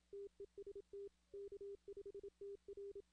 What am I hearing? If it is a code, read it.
Morse code, heard as a keyed tone, about 27 wpm, 400 Hz. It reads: TEST K5TR
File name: 2016 arrl dx cw / 2016-ARRL-DX-CW-K5TR- - 0-41